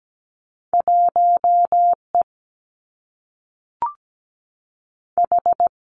Morse code, heard as a keyed tone, about 17 wpm, 700 Hz. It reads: 1 E H
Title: Two Challenging Letter-Number Combinations 17wpm